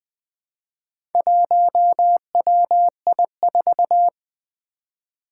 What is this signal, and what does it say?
Morse code, heard as a keyed tone, about 20 wpm, 700 Hz. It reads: 1WI4
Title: Four Challenging Letter-Number Combinations 20wpm